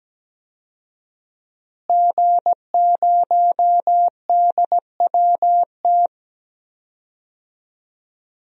Morse code, heard as a keyed tone, about 17 wpm, 700 Hz. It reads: G0DWT
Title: DXCC-100 Call Signs - 5 Characters - Repeated 2x - 17wpm